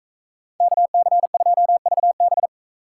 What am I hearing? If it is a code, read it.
Morse code, heard as a keyed tone, about 44 wpm, 700 Hz. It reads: KC2VB